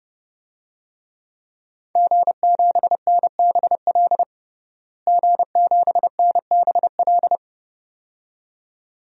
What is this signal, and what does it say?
Morse code, heard as a keyed tone, about 30 wpm, 700 Hz. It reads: G7NBL G7NBL
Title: DXCC-100 Call Signs - 5 Characters - Repeated 2x - 30wpm